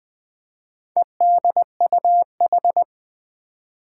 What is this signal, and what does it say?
Morse code, heard as a keyed tone, about 21 wpm, 700 Hz. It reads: EDUH
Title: Four Challenging Letter-Number Combinations 20wpm